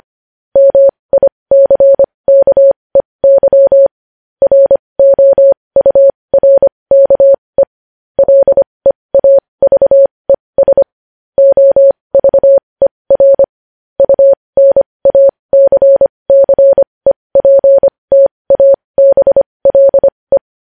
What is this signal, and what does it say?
Morse code, heard as a keyed tone, about 25 wpm, spaced as for 20 wpm, 550 Hz. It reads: MICKEY ROURKE LEAVES OVER UNACCEPTABLE
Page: News Headlines in Morse Code at 20 WPM